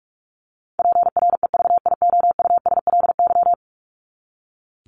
Morse code, heard as a keyed tone, about 44 wpm, 700 Hz. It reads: PREVIOUSLY
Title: LY Suffix - Top 100 Words with Base Words 45wpm